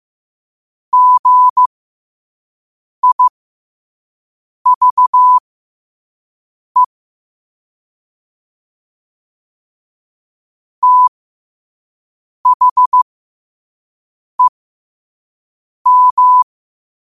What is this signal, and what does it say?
Morse code, heard as a keyed tone, about 15 wpm, spaced as for 5 wpm, 1000 Hz. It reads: GIVE THEM
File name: qotdfiles / 5wpm